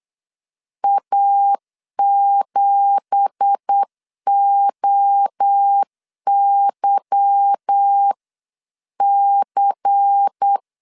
Morse code, heard as a keyed tone, about 8 wpm, 800 Hz. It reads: A7OYC
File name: 8wpm